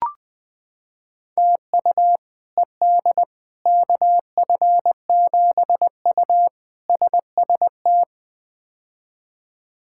Morse code, heard as a keyed tone, about 21 wpm, 700 Hz. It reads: TU ED KF7U SST
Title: K1USN SST QSO Practice - 20wpm